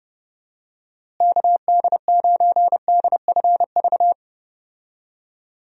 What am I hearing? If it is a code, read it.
Morse code, heard as a keyed tone, about 30 wpm, 700 Hz. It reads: KD9DFV
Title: USA Call Signs - 6 Characters - Repeated 3x - 30wpm